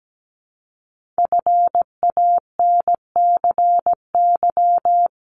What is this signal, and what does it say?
Morse code, heard as a keyed tone, about 17 wpm, 700 Hz. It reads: FANCY
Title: Multi-letter Phonemes 17wpm